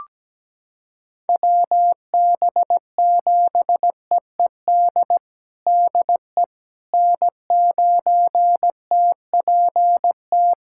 Morse code, heard as a keyed tone, about 17 wpm, 700 Hz. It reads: WB7EED DE N9TPT
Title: USA Callsign DE Callsign 17wpm